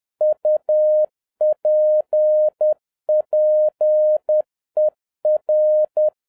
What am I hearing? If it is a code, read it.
Morse code, heard as a keyed tone, about 10 wpm, 610 Hz. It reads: UPPER